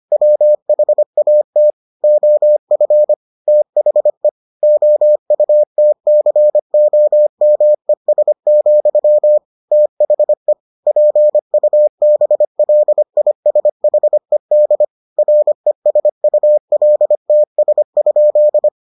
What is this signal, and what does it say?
Morse code, heard as a keyed tone, about 25 wpm, 600 Hz. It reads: WHAT OF THE OUTCOMES, THE PUBLISHED RESULTS?